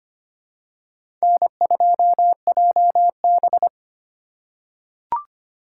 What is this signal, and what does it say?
Morse code, heard as a keyed tone, about 25 wpm, 700 Hz. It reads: N2JB
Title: USA Call Signs - 4 Characters - 8x Character Spacing - 25wpm